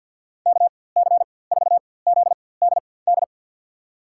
Morse code, heard as a keyed tone, about 52 wpm, spaced as for 24 wpm, 700 Hz. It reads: KC4ZDD
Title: USA Call Signs - 6 Characters - 4x Character Spacing - 50wpm